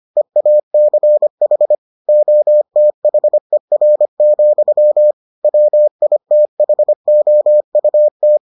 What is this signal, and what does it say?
Morse code, heard as a keyed tone, about 25 wpm, 600 Hz. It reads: EACH OTHER, WITHOUT